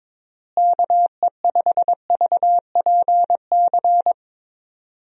Morse code, heard as a keyed tone, about 22 wpm, 700 Hz. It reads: KE5VPC
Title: USA Call Signs - 6 Characters - Repeated 3x - 22wpm